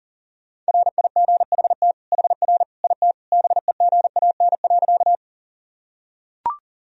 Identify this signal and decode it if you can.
Morse code, heard as a keyed tone, about 40 wpm, 700 Hz. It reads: RIGHT HR IT BEGAN.
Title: Sentences from Top 300 Words and Abbreviations 40wpm